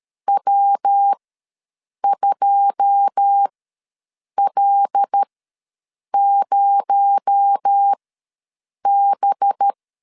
Morse code, heard as a keyed tone, about 13 wpm, spaced as for 7 wpm, 800 Hz. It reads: W2L0B
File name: LEZIONI / 9 2 ws 12wpm 3406